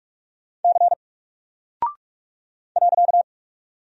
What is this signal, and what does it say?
Morse code, heard as a keyed tone, about 46 wpm, 700 Hz. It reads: C .
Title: Single Character (Letter:Number:Punctuation:Prosign) Alt 45wpm